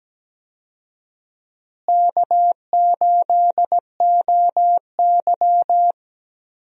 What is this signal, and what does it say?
Morse code, heard as a keyed tone, about 17 wpm, 700 Hz. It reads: K8OY
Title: USA Call Signs - 4 Characters - Repeated 3x - 17wpm